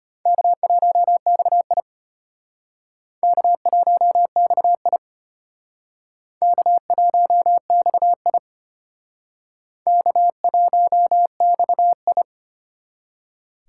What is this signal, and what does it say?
Morse code, heard as a keyed tone, about 34 wpm, 700 Hz. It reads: K1XI K1XI K1XI K1XI